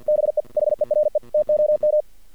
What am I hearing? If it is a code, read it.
Morse code, heard as a keyed tone, about 50 wpm, 610 Hz. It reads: BE HERE TOM
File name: archived / CRSnet